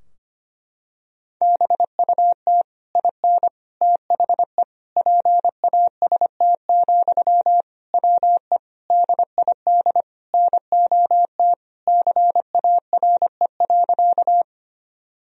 Morse code, heard as a keyed tone, about 25 wpm, 700 Hz. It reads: BUT IN THE PAST, WE DID NOT CARE.
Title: Sentences from Top 1000 Words - Review Part 3 - 25wpm